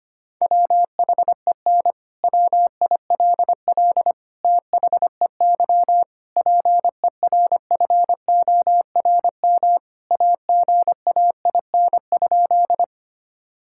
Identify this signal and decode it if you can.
Morse code, heard as a keyed tone, about 25 wpm, 700 Hz. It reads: WHEN WILL THEY PERFORM AGAIN?
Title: Sentences from Top 1000 Words - Review Part 5 - 25wpm